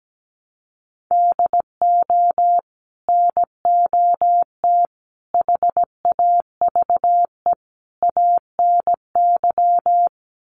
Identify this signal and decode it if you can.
Morse code, heard as a keyed tone, about 17 wpm, 700 Hz. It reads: DO NOT HAVE ANY
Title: Sets of 4 Words 17wpm